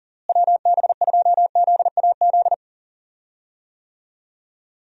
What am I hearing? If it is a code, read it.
Morse code, heard as a keyed tone, about 40 wpm, 700 Hz. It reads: WB2ZAZ
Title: USA Call Signs - 6 Characters - Repeated 3x - 40wpm